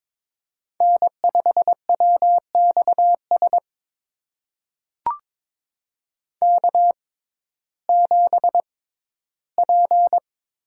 Morse code, heard as a keyed tone, about 22 wpm, spaced as for 14 wpm, 700 Hz. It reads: N5WXS K7P
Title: USA Call Signs - 5 Characters - 6x Character Spacing - 22wpm